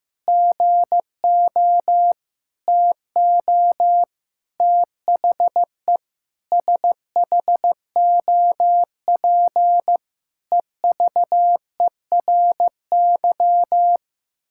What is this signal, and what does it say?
Morse code, heard as a keyed tone, about 15 wpm, 700 Hz. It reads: GO TO THE SHOP EVERY
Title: Sentences from Top 1600 Words 15wpm